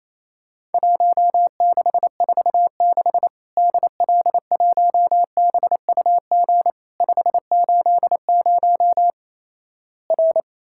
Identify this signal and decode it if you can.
Morse code, heard as a keyed tone, about 28 wpm, 700 Hz. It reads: 1646 DL1BUG 580 R
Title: Worked All Europe DX Contest - QTC Exchange - 28wpm